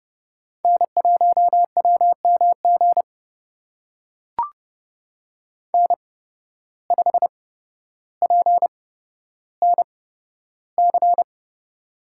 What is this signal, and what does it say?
Morse code, heard as a keyed tone, about 30 wpm, spaced as for 13 wpm, 700 Hz. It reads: N1WMG N5PNC
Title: USA Call Signs - 5 Characters - 8x Character Spacing - 30wpm